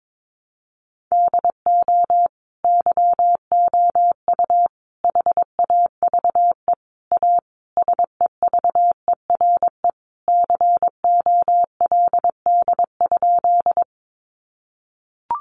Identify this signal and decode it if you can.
Morse code, heard as a keyed tone, about 22 wpm, 700 Hz. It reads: DO YOU HAVE A SEVERE COLD?
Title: Sentences from Top 1400 Words 22wpm